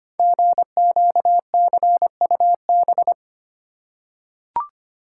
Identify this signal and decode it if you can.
Morse code, heard as a keyed tone, about 25 wpm, 700 Hz. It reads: GQCUB
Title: Code Groups - 5 Letter:Numbers - 25wpm